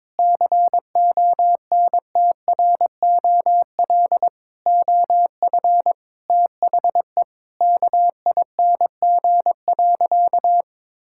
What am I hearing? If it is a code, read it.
Morse code, heard as a keyed tone, about 21 wpm, 700 Hz. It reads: CONTROL OF THE KING.
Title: Sentences from Top 1000 Words - Review Part 5 - 22wpm